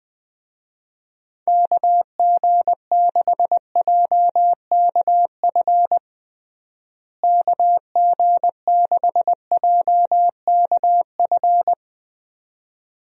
Morse code, heard as a keyed tone, about 20 wpm, 700 Hz. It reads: KG6JKF KG6JKF
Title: USA Call Signs - 6 Characters - Repeated 2x - 20wpm